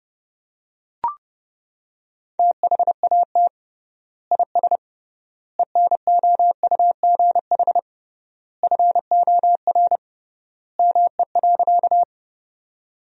Morse code, heard as a keyed tone, about 30 wpm, 700 Hz. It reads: THAT IS ENOUGH FOR ME.
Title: 3x Word-Spacing - Sentences from Top 400 Words - 30wpm